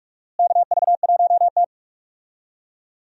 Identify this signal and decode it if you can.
Morse code, heard as a keyed tone, about 42 wpm, 700 Hz. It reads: KV1T
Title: USA Call Signs - 4 Characters - Repeated 3x - 45wpm